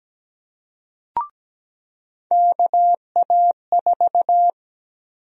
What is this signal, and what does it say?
Morse code, heard as a keyed tone, about 17 wpm, 700 Hz. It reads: KA4
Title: USA Call Sign Prefixes - 17wpm